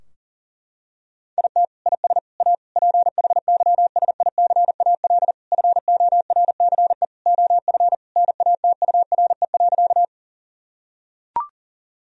Morse code, heard as a keyed tone, about 40 wpm, 700 Hz. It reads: IT IS A PHYSICAL FORCE OF NATURE.
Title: Sentences from Top 1000 Words - Review Part 4 - 40wpm